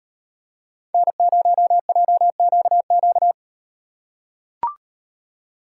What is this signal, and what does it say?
Morse code, heard as a keyed tone, about 38 wpm, 700 Hz. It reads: N0JQQ